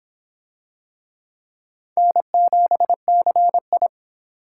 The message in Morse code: N7CI

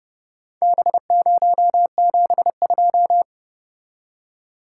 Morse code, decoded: B072

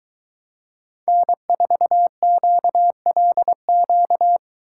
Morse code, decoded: N4QLQ

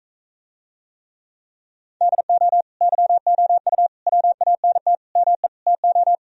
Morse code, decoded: DO YOU WANT ME TO

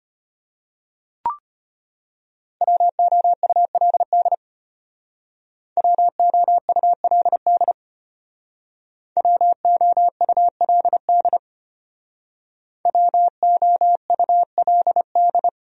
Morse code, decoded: WOULD WOULD WOULD WOULD

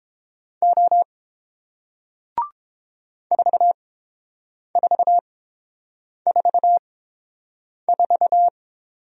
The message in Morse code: O 4444

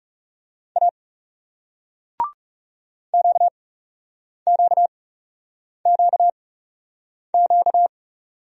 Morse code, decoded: A QQQQ

